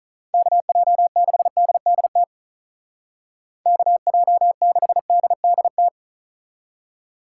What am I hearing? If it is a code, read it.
Morse code, heard as a keyed tone, about 44 wpm, 700 Hz. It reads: KJ6DDT KJ6DDT
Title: Speed Racing - USA 2x3 Call Signs 35wpm